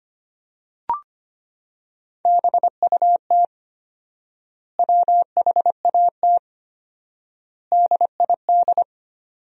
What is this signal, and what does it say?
Morse code, heard as a keyed tone, about 25 wpm, 700 Hz. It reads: BUT WHAT DID
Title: 4x Word-Spacing - Sentences from Top 100 Words - 25wpm